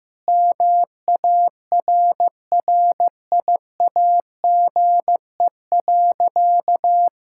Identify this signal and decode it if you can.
Morse code, heard as a keyed tone, about 15 wpm, 700 Hz. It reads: MARRIAGE.
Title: Sentences from Top 1000 Words - Review Part 5 - 15wpm